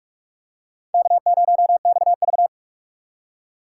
Morse code, heard as a keyed tone, about 44 wpm, 700 Hz. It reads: K0XV